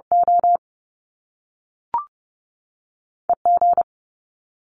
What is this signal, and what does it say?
Morse code, heard as a keyed tone, about 30 wpm, 700 Hz. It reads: O E G
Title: Top Two Letter Combinations in Words 30wpm